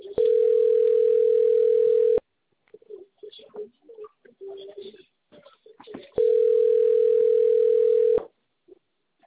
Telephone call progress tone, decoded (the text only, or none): ringback tone